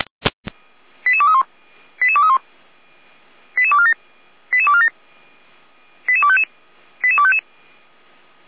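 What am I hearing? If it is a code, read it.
Five-tone selective call (ZVEI), heard as quick runs of five tones, tones of about 80 ms. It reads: 88231 88231 88247 88247 88255 88255
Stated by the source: Five-tone ZVEI selective call received on SDR.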